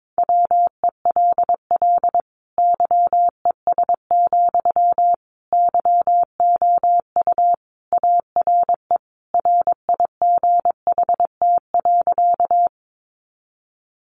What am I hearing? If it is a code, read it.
Morse code, heard as a keyed tone, about 22 wpm, 700 Hz. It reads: WELL YES, YOU ARE RIGHT.
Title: Sentences from Top 500 Words 22wpm